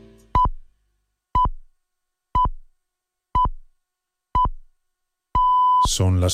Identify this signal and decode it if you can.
Radio time signal (time pips), with speech. It is heard from 0.3 to 5.8 s.